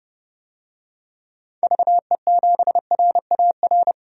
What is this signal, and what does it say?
Morse code, heard as a keyed tone, about 30 wpm, 700 Hz. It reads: VE7RAR
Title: DXCC-100 Call Signs - 6 Characters - 30wpm